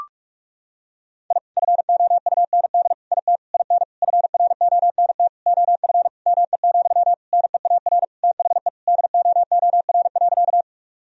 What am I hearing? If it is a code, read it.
Morse code, heard as a keyed tone, about 46 wpm, 700 Hz. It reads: I FOUND IT IN FRONT OF ME, NEAR THE DOOR.
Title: Sentences from Top 1000 Words - Review Part 2 - 45wpm